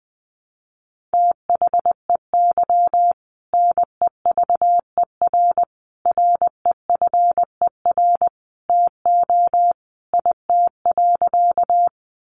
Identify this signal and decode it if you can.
Morse code, heard as a keyed tone, about 20 wpm, 700 Hz. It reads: THEY NEVER REFER TO IT.
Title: Sentences from Top 1000 Words 20wpm